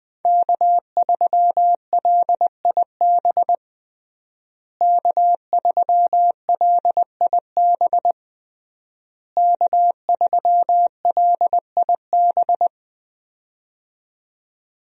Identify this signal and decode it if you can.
Morse code, heard as a keyed tone, about 20 wpm, 700 Hz. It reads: K3LIB K3LIB K3LIB
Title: USA Call Signs - 5 Characters - Repeated 3x - 20wpm